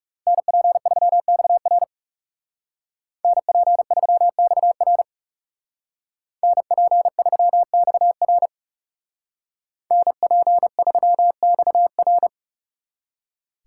Morse code, heard as a keyed tone, about 34 wpm, 700 Hz. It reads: NP3XR NP3XR NP3XR NP3XR
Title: Speed Racing - USA Call Signs 30wpm